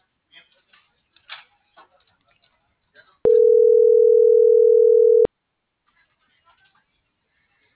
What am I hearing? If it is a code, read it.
Ringback tone (North America), 440 + 480 Hz.